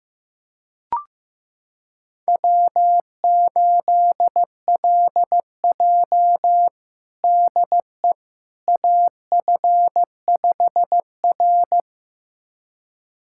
Morse code, heard as a keyed tone, about 15 wpm, 700 Hz. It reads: W8LJ DE AF5R
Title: USA Callsign DE Callsign - 4 Characters - 15wpm